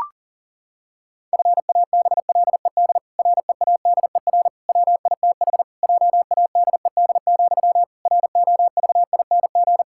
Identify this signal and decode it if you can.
Morse code, heard as a keyed tone, about 40 wpm, 700 Hz. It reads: FABLED READER WITH JADED, ROVING
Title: Pangram Challenge v2 - 40wpm